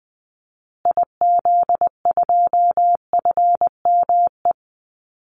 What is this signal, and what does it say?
Morse code, heard as a keyed tone, about 19 wpm, 700 Hz. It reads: IZ2FME
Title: Call Signs Repeated 2x - 20wpm